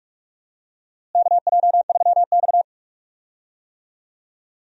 Morse code, heard as a keyed tone, about 46 wpm, 700 Hz. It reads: KJ3X
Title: USA Call Signs - 4 Characters - Repeated 2x - 45wpm